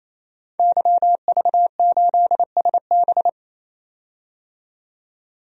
Morse code, heard as a keyed tone, about 28 wpm, 700 Hz. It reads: YV8SB
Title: DXCC-100 Call Signs - 5 Characters - 28wpm